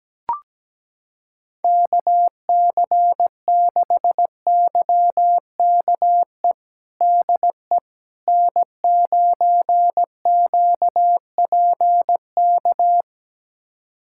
Morse code, heard as a keyed tone, about 17 wpm, 700 Hz. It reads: KC6YKE DE N9QPK
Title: USA Callsign DE Callsign 17wpm